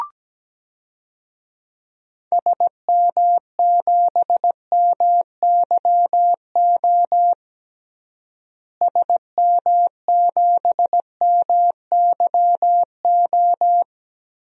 Morse code, heard as a keyed tone, about 17 wpm, 700 Hz. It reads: SM7MYO SM7MYO
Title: DXCC-100 Call Signs - 6 Characters - Repeated 2x - 17wpm